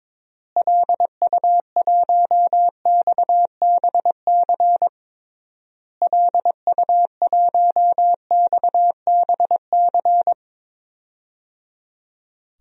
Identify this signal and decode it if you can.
Morse code, heard as a keyed tone, about 22 wpm, 700 Hz. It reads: LU1XBC LU1XBC
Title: DXCC-100 Call Signs - 6 Characters - Repeated 3x - 22wpm